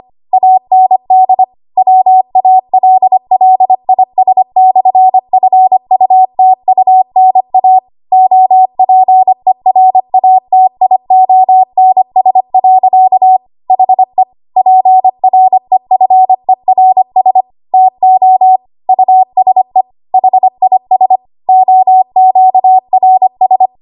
Morse code, heard as a keyed tone, about 25 wpm, 750 Hz. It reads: AND WALLIS/FUTUNA OPERATIONS. HE PREFERS TO USE HIS OQRS
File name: Archive / 25 WPM / 25WPM